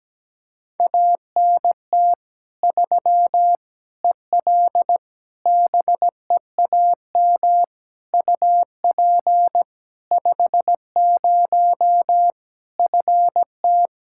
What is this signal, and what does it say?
Morse code, heard as a keyed tone, about 17 wpm, 700 Hz. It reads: ANT 3 EL BEAM UP 50 FT
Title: Practice QSO - Exchange 2 - Info - 17wpm